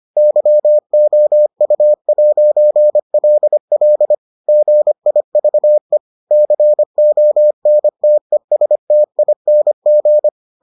Morse code, heard as a keyed tone, about 25 wpm, 600 Hz. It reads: YOU'LL GIVE CONTESTING